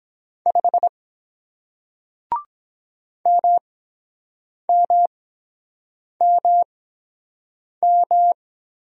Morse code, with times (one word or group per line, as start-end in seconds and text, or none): 0.5-0.9 5
3.3-8.3 MMMM